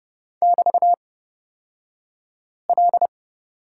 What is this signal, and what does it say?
Morse code, heard as a keyed tone, about 29 wpm, spaced as for 5 wpm, 700 Hz. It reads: <BT>L
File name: Lesson 36d - BT - Two Characters 30wpm